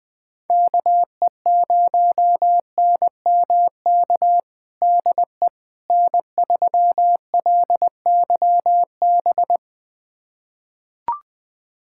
Morse code, heard as a keyed tone, about 20 wpm, 700 Hz. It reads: KE0NMK DE N3LYB
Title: USA Callsign DE Callsign 20wpm